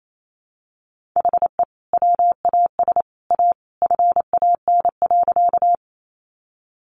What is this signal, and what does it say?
Morse code, heard as a keyed tone, about 28 wpm, 700 Hz. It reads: HE WAS A FAN.
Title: Sentences from Top 1300 Words 28wpm